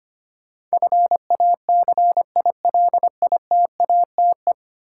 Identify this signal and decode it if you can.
Morse code, heard as a keyed tone, about 25 wpm, 700 Hz. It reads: FACILITATE
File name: practice / 25wpm / Ten Letter Words - 25wpm